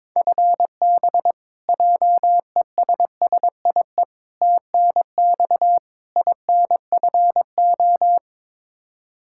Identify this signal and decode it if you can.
Morse code, heard as a keyed tone, about 22 wpm, 700 Hz. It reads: FB JESSIE TNX INFO
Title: Practice QSO - Exchange 3 - More Info - 22wpm